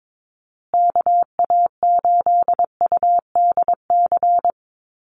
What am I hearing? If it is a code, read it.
Morse code, heard as a keyed tone, about 22 wpm, 700 Hz. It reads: KA8UDC
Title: USA Call Signs Repeated 2x 22wpm